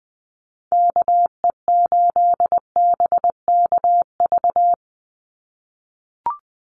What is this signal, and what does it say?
Morse code, heard as a keyed tone, about 20 wpm, 700 Hz. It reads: KE8BKV
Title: USA Call Signs 20wpm